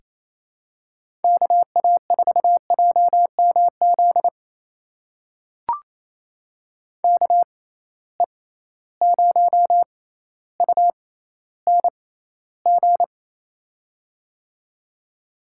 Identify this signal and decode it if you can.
Morse code, heard as a keyed tone, about 28 wpm, spaced as for 14 wpm, 700 Hz. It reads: KA4JMZ KE0UNG
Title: USA Call Signs - 6 Characters - 6x Character Spacing - 28wpm